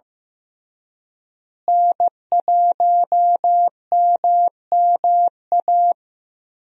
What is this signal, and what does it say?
Morse code, heard as a keyed tone, about 15 wpm, 700 Hz. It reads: N1MMA
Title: USA Call Signs - 5 Characters - Repeated 2x - 15wpm